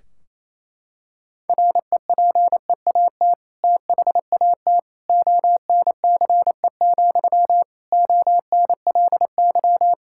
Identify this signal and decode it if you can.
Morse code, heard as a keyed tone, about 28 wpm, 700 Hz. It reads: REPEAT THAT ONCE, ONLY
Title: Sentences from Top 1600 Words 28wpm